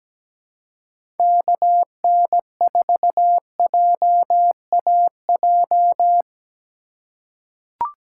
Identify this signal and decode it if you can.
Morse code, heard as a keyed tone, about 17 wpm, 700 Hz. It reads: KN4JAJ
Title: USA Call Signs - 6 Characters - 3x Character Spacing - 17wpm